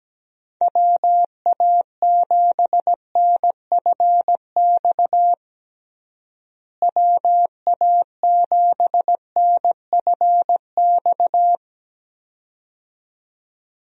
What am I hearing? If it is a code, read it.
Morse code, heard as a keyed tone, about 17 wpm, 700 Hz. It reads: WA7NFX WA7NFX
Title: USA Call Signs - 6 Characters - Repeated 2x - 17wpm